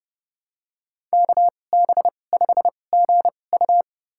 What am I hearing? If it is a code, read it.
Morse code, heard as a keyed tone, about 30 wpm, spaced as for 22 wpm, 700 Hz. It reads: KB5GU
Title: USA Call Signs - 5 Characters - 2x Character Spacing - 30wpm